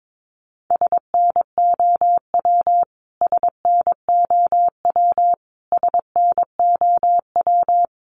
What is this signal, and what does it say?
Morse code, heard as a keyed tone, about 22 wpm, 700 Hz. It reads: SNOW SNOW SNOW
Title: Top 500 Words Repeated 22wpm